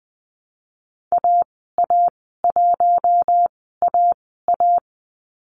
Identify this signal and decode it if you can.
Morse code, heard as a keyed tone, about 20 wpm, spaced as for 15 wpm, 700 Hz. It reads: AA1AA
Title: 088 Lesson 15g - Number 5 - Call Signs 20wpm